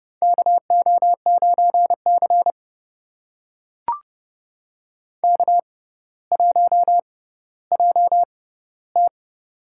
Morse code, heard as a keyed tone, about 31 wpm, spaced as for 15 wpm, 700 Hz. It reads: KO9C K1JT
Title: USA Call Signs - 4 Characters - 6x Character Spacing - 30wpm